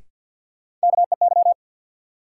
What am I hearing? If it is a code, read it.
Morse code, heard as a keyed tone, about 47 wpm, 700 Hz. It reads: KEY
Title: Sentences from Top 600 Words - New Word Drill - 50wpm